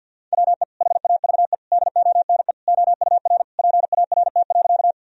Morse code, heard as a keyed tone, about 48 wpm, 700 Hz. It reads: WE HAVE DONE OUR PART.
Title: Sentences from Top 1000 Words - Review Part 1 - 50wpm